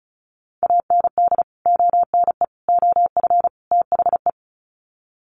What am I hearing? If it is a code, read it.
Morse code, heard as a keyed tone, about 34 wpm, 700 Hz. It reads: AND ONE OF THE
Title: Sets of 4 Words 35wpm